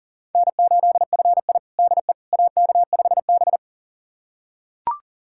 Morse code, heard as a keyed tone, about 37 wpm, 700 Hz. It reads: N8FI DE AK5B
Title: USA Callsign DE Callsign - 4 Characters - 40wpm